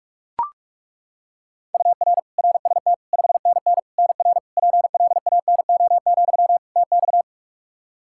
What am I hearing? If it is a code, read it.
Morse code, heard as a keyed tone, about 42 wpm, 700 Hz. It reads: UR RST 5NN NR PLANO, TX
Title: Practice QSO - Exchange 1 - Beginning - 45wpm